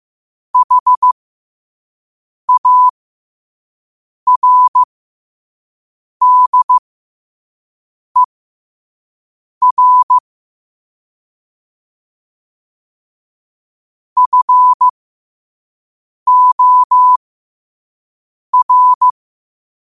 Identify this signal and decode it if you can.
Morse code, heard as a keyed tone, about 15 wpm, spaced as for 5 wpm, 1000 Hz. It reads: HARDER FOR
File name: qotdfiles / 5wpm